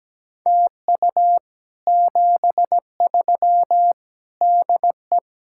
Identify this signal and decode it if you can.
Morse code, heard as a keyed tone, about 17 wpm, 700 Hz. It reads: TU 73 DE
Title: Speed-Racing Common QSO Phrases - 17wpm